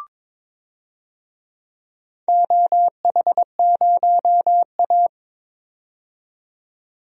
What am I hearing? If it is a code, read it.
Morse code, heard as a keyed tone, about 23 wpm, 700 Hz. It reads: OH0A
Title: DXCC-100 Call Signs - 4 Characters - 22wpm